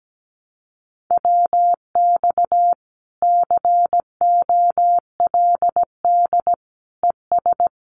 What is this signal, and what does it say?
Morse code, heard as a keyed tone, about 17 wpm, 700 Hz. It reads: WX COLD ES
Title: Common QSO Phrases 17wpm